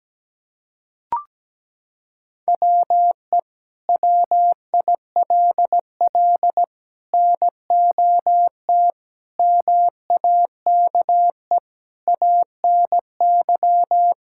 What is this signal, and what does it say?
Morse code, heard as a keyed tone, about 17 wpm, 700 Hz. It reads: WE WILL NOT MAKE ANY
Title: Sentences from Top 1000 Words - Review Part 5 - 17wpm